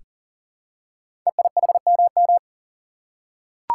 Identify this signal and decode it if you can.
Morse code, heard as a keyed tone, about 39 wpm, 700 Hz. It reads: EIHMM
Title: Code Groups - 5 Letters - 40wpm